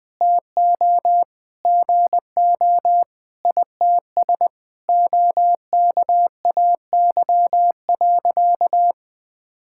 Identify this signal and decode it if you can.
Morse code, heard as a keyed tone, about 20 wpm, 700 Hz. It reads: TO GO ITS OKAY.
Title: Sentences with 5-Word Phrases - 20wpm